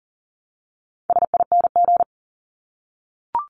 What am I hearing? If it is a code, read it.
Morse code, heard as a keyed tone, about 38 wpm, 700 Hz. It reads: SING
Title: Top 500 Words Repeated 40wpm